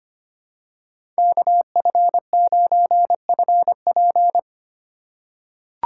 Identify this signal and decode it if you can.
Morse code, heard as a keyed tone, about 25 wpm, 700 Hz. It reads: KF9FP
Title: USA Call Signs - 5 Characters - 3x Character Spacing - 25wpm